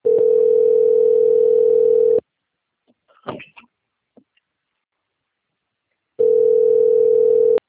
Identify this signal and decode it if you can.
Ringback tone (North America), 440 + 480 Hz.